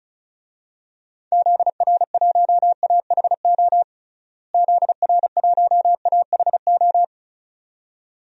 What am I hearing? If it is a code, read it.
Morse code, heard as a keyed tone, about 36 wpm, 700 Hz. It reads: ZR1AHO ZR1AHO
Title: DXCC-100 Call Signs - 6 Characters - Repeated 2x - 35wpm